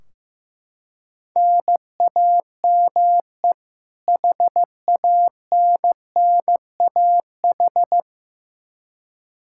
Morse code, heard as a keyed tone, about 15 wpm, 700 Hz. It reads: NAME HANNAH
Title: Practice QSO - Exchange 1 - Beginning - 15wpm